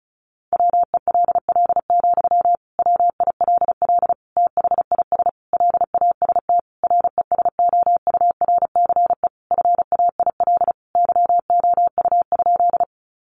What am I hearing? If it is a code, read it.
Morse code, heard as a keyed tone, about 34 wpm, 700 Hz. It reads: WELL, WILL THIS LAST RESOURCE FAIL YOU?
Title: Sentences from Top 1300 Words 35wpm